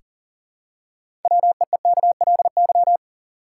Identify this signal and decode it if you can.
Morse code, heard as a keyed tone, about 40 wpm, 700 Hz. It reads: WEEKLY
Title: Six Letter Words - 40wpm